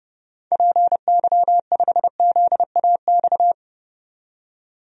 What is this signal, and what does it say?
Morse code, heard as a keyed tone, about 32 wpm, 700 Hz. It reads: PY5ZAX
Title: DXCC-100 Call Signs - 6 Characters - 30wpm